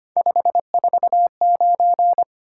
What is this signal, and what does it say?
Morse code, heard as a keyed tone, about 25 wpm, 700 Hz. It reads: 549
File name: Three Letter-Number Combinations - 25wpm